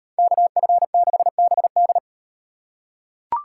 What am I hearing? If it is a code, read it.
Morse code, heard as a keyed tone, about 38 wpm, 700 Hz. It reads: KF6BD